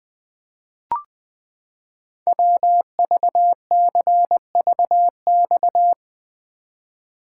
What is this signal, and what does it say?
Morse code, heard as a keyed tone, about 20 wpm, 700 Hz. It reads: WVCVX
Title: Code Groups - 5 Letter:Numbers - 20wpm